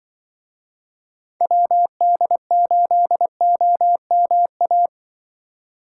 Morse code, heard as a keyed tone, about 24 wpm, 700 Hz. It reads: WD8OMA
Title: Speed Racing - USA Call Signs 20wpm